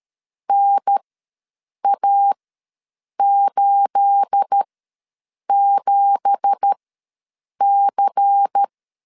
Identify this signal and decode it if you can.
Morse code, heard as a keyed tone, about 13 wpm, spaced as for 7 wpm, 800 Hz. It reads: NA87C
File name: LEZIONI / 9 2867 ws 12wpm 3431